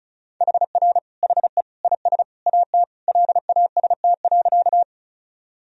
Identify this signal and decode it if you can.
Morse code, heard as a keyed tone, about 36 wpm, 700 Hz. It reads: HR HE IS AT LAST.
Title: Sentences from Top 300 Words and Abbreviations 35wpm